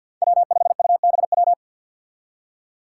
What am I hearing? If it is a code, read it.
Morse code, heard as a keyed tone, about 52 wpm, 700 Hz. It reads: W5UBW
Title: Five Challenging Letter-Number Combinations 50wpm